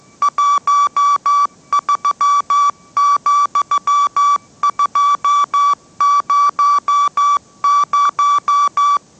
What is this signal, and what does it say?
Morse code, heard as a keyed tone, about 15 wpm, 1150 Hz. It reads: 13,200
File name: mrx / mrx065 16wpm